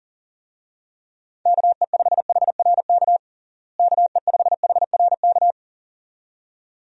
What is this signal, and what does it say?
Morse code, heard as a keyed tone, about 40 wpm, 700 Hz. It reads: KE5HRK KE5HRK
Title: USA Call Signs - 6 Characters - Repeated 2x - 40wpm